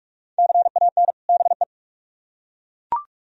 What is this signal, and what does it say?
Morse code, heard as a keyed tone, about 39 wpm, 700 Hz. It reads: CAN BE